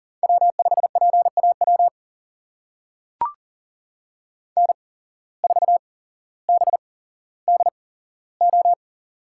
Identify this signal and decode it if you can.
Morse code, heard as a keyed tone, about 42 wpm, spaced as for 17 wpm, 700 Hz. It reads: W5PAW N4BDO
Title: USA Call Signs - 5 Characters - 8x Character Spacing - 40wpm